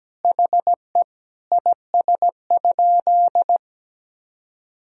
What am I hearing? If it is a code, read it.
Morse code, heard as a keyed tone, about 17 wpm, 700 Hz. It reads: HE IS?
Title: Sentences from Top 300 Words and Abbreviations 17wpm